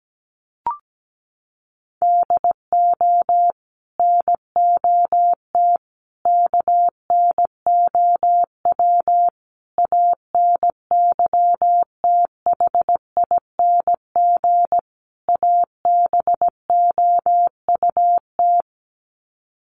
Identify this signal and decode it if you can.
Morse code, heard as a keyed tone, about 17 wpm, 700 Hz. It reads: DO NOT KNOW ANYTHING ABOUT
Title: Sets of 5 Words 17wpm